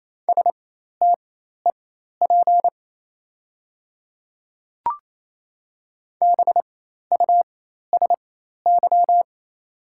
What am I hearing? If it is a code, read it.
Morse code, heard as a keyed tone, about 28 wpm, spaced as for 13 wpm, 700 Hz. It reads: STEP BUSY
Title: 4x Character Spacing - Top 500 Words - 28wpm